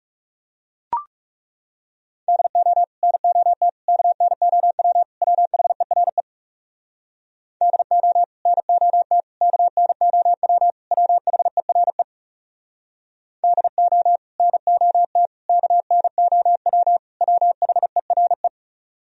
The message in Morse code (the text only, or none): DO NOT KNOW WHERE DO NOT KNOW WHERE DO NOT KNOW WHERE